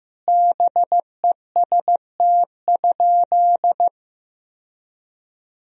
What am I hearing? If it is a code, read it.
Morse code, heard as a keyed tone, about 15 wpm, 700 Hz. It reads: BEST?
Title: Sentences from Top 1000 Words - Review Part 1 - 15wpm